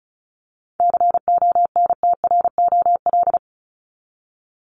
Morse code, heard as a keyed tone, about 34 wpm, 700 Hz. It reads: CONTROL
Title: Top 501-750 Words 35wpm